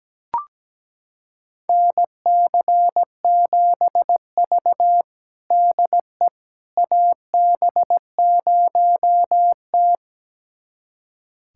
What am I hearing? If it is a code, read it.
Morse code, heard as a keyed tone, about 17 wpm, 700 Hz. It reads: NC7V DE AB0T
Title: USA Callsign DE Callsign - 4 Characters - 17wpm